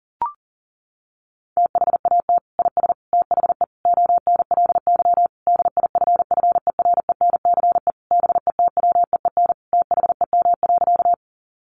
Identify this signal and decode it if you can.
Morse code, heard as a keyed tone, about 40 wpm, 700 Hz. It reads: THAT IS THE ONLY DIFFERENCE BETWEEN THEM.
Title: Sentences from Top 800 Words 40wpm